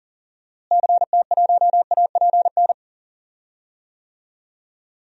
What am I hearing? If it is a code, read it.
Morse code, heard as a keyed tone, about 39 wpm, 700 Hz. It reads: CT1APN
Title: DXCC-100 Call Signs - 6 Characters - Repeated 2x - 40wpm